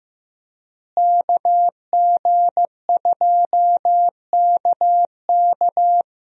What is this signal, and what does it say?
Morse code, heard as a keyed tone, about 15 wpm, 700 Hz. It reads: KG2KK
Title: USA Call Signs - 5 Characters - Repeated 3x - 15wpm